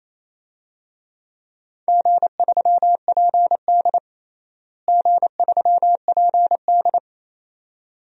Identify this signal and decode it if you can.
Morse code, heard as a keyed tone, about 28 wpm, 700 Hz. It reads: G3PD G3PD
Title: DXCC-100 Call Signs - 4 Characters - Repeated 2x - 28wpm